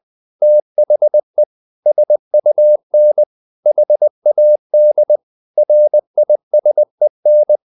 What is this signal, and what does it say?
Morse code, heard as a keyed tone, about 20 wpm, 600 Hz. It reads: THE SUN HAD RISEN
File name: the old man and the sea / the old man and the sea part 1 20wpm